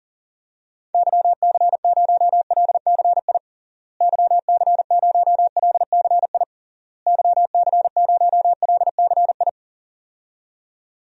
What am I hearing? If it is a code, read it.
Morse code, heard as a keyed tone, about 40 wpm, 700 Hz. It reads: YC0LCI YC0LCI YC0LCI
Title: DXCC-100 Call Signs - 6 Characters - Repeated 3x - 40wpm